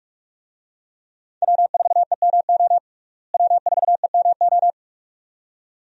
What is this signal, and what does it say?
Morse code, heard as a keyed tone, about 46 wpm, 700 Hz. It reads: W4EMO W4EMO
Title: USA Call Signs - 5 Characters - Repeated 2x - 45wpm